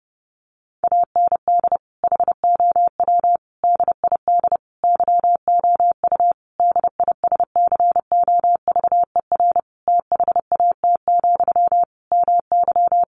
Morse code, heard as a keyed tone, about 30 wpm, 700 Hz. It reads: AND HOW DID YOU DISCOVER THAT, MY
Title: Sentences from Top 1100 Words 30wpm